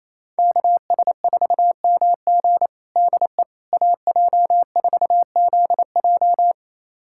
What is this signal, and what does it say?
Morse code, heard as a keyed tone, about 28 wpm, 700 Hz. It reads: KS4MG DE AJ4ZJ
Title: USA Callsign DE Callsign - 5 Characters - 28wpm